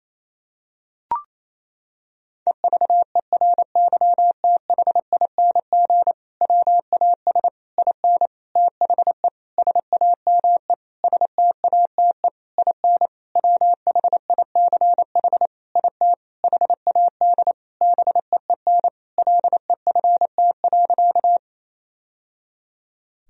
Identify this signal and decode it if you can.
Morse code, heard as a keyed tone, about 27 wpm, 700 Hz. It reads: EVERYTHING WAS IN THE SAME STATE IN WHICH IT HAD BEEN LEFT.
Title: Sentences from Top 1000 Words - Review Part 2 - 28wpm